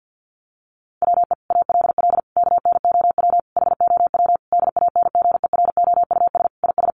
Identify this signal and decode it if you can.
Morse code, heard as a keyed tone, about 50 wpm, 700 Hz. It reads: WE ALL KNOW HOW DANGEROUS IS
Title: Sentences from Top 1200 Words 50wpm